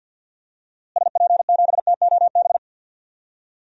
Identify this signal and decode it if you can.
Morse code, heard as a keyed tone, about 47 wpm, 700 Hz. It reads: SP7TOB
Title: DXCC-100 Call Signs - 6 Characters - Repeated 3x - 50wpm